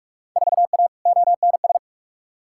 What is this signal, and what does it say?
Morse code, heard as a keyed tone, about 45 wpm, 700 Hz. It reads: 4A ONS
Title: Field Day Exchanges - 45wpm